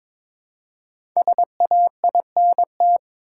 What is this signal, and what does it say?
Morse code, heard as a keyed tone, about 22 wpm, 700 Hz. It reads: SAINT
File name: 5 Letter Words - First 3 Letters Given - 22wpm